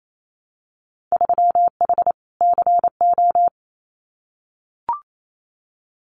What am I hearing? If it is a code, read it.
Morse code, heard as a keyed tone, about 28 wpm, 700 Hz. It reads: 3H CO
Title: Winter Field Day Exchanges 28wpm